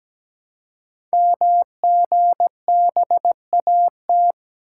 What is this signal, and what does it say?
Morse code, heard as a keyed tone, about 17 wpm, 700 Hz. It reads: MGBAT
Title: Code Groups - 5 Letter:Numbers - 17wpm